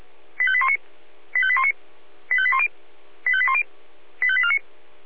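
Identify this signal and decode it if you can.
Five-tone selective call (ZVEI), heard as quick runs of five tones, tones of about 70 ms. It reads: 85819 85819 85810 85810 85849
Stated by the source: ZVEI five-tone selective call received on an SDR.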